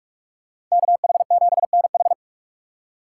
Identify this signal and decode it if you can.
Morse code, heard as a keyed tone, about 46 wpm, 700 Hz. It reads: KH7NH
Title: USA Call Signs - 2x Character Spacing - 45wpm